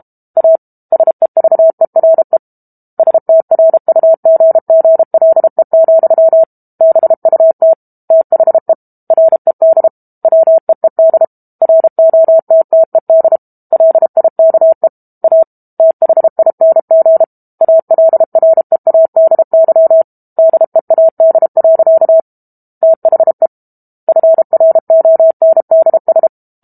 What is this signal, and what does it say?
Morse code, heard as a keyed tone, about 32 wpm, 650 Hz. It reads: A SEVERE STRUGGLE, BUT THE RED WEED ROTTED LIKE A THING ALREADY DEAD. THE FRONDS
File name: Book Two / 06-Chapter 6 - 32 WPM